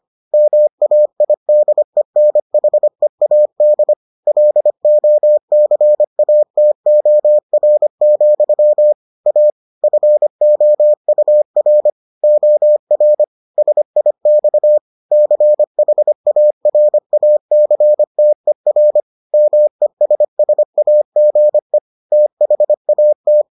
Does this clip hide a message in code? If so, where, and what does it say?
Yes: Morse code from 0.3 to 23.4 s, heard as a keyed tone, about 25 wpm, 600 Hz. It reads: MAIDENHEAD LOCATOR, A FOUR OR SIX CHARACTER MESSAGE THAT